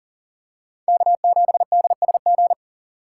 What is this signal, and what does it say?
Morse code, heard as a keyed tone, about 40 wpm, 700 Hz. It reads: K7DSG
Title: USA Call Signs - 5 Characters - 4x Character Spacing - 40wpm